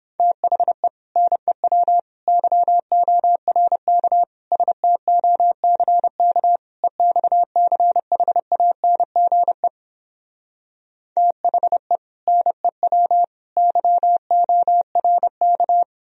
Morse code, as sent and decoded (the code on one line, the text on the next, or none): - .... / -. .-- / -.-- --- .-. -.- / ... - --- -.-. -.- / . -..- -.-. .... .- -. --. / - .... / -. .-- / -.-- --- .-. -.-
THE NEW YORK STOCK EXCHANGE THE NEW YORK